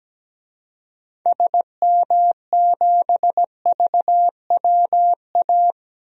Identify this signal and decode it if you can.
Morse code, heard as a keyed tone, about 17 wpm, 700 Hz. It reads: SM7VWA
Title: DXCC-100 Call Signs - 6 Characters - 17wpm